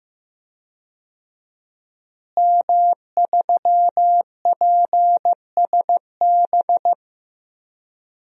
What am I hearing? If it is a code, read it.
Morse code, heard as a keyed tone, about 15 wpm, 700 Hz. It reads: M3PSB
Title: DXCC-100 Call Signs - 5 Characters - Repeated 3x - 15wpm